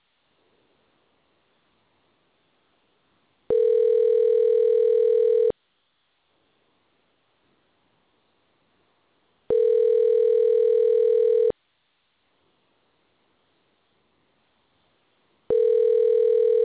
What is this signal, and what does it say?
Ringback tone (North America), 440 + 480 Hz.